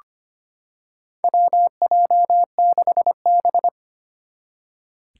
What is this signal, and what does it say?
Morse code, heard as a keyed tone, about 25 wpm, 700 Hz. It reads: WJ6B
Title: Four Challenging Letter-Number Combinations 25wpm